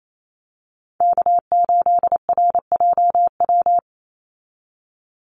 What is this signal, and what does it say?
Morse code, heard as a keyed tone, about 28 wpm, 700 Hz. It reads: K8RJW
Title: Call Signs - 28wpm